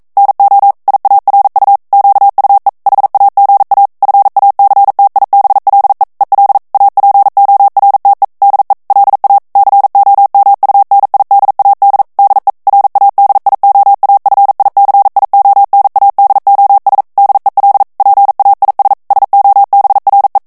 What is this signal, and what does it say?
Morse code, heard as a keyed tone, about 42 wpm, 800 Hz. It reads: NO IARU QUE HAGA FACTIBLE EL APORTE DE LA COMUNIDAD DE RADIOAFICIONADOS DEL PAIS SOBRE
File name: morse mp3 / ESPAGNOL TEXTE EN CLAIR 1 VITESSE 40 MOTS MINUTES